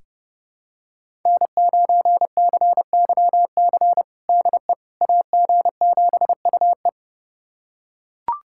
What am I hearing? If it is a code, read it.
Morse code, heard as a keyed tone, about 30 wpm, 700 Hz. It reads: N9CYC DE AG7UE
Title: USA Callsign DE Callsign - 5 Characters - 30wpm